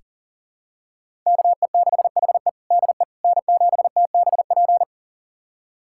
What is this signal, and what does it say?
Morse code, heard as a keyed tone, about 40 wpm, 700 Hz. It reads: KE6HE DE N7TBP
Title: USA Callsign DE Callsign - 5 Characters - 40wpm